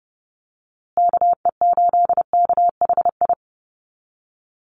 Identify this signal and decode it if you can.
Morse code, heard as a keyed tone, about 30 wpm, 700 Hz. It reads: KE8KHI
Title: USA Call Signs Repeated 2x 30wpm